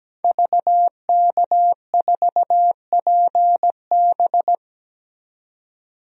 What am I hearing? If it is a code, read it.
Morse code, heard as a keyed tone, about 17 wpm, 700 Hz. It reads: VK4PB
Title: DXCC-100 Call Signs - 5 Characters - 17wpm